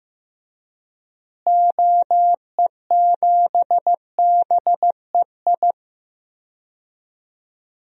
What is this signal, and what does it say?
Morse code, heard as a keyed tone, about 15 wpm, 700 Hz. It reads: OE7BEI
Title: DXCC-100 Call Signs - 6 Characters - Repeated 2x - 15wpm